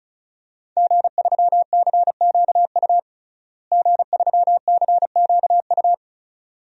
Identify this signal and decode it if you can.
Morse code, heard as a keyed tone, about 36 wpm, 700 Hz. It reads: G3CQU G3CQU
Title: DXCC-100 Call Signs - 5 Characters - Repeated 2x - 35wpm